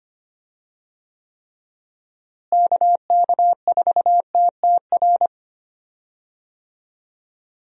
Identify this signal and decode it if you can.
Morse code, heard as a keyed tone, about 25 wpm, 700 Hz. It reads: KK4TTR
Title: USA Call Signs - 6 Characters - Repeated 2x - 25wpm